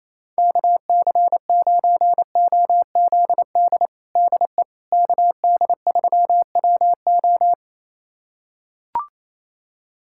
Morse code, heard as a keyed tone, about 28 wpm, 700 Hz. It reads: KC9OZD DE KD3WO
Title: USA Callsign DE Callsign 28wpm